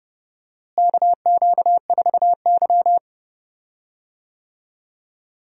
Morse code, heard as a keyed tone, about 32 wpm, 700 Hz. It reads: KQ4Y
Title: USA Call Signs - 4 Characters - Repeated 2x - 30wpm